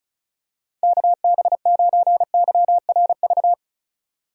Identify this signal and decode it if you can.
Morse code, heard as a keyed tone, about 34 wpm, 700 Hz. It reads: KB9YRV
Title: USA Call Signs - 6 Characters - 35wpm